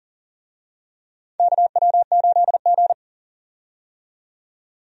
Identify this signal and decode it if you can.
Morse code, heard as a keyed tone, about 40 wpm, 700 Hz. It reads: KW8G
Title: USA Call Signs - 4 Characters - 40wpm